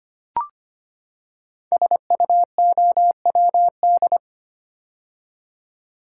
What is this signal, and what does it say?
Morse code, heard as a keyed tone, about 25 wpm, 700 Hz. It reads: SUOWD
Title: Code Groups - 5 Letters - 25wpm